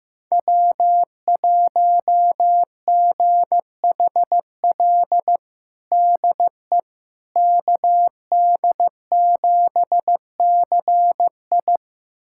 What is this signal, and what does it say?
Morse code, heard as a keyed tone, about 15 wpm, 700 Hz. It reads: W1GHL DE KD7CI
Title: USA Callsign DE Callsign - 5 Characters - 15wpm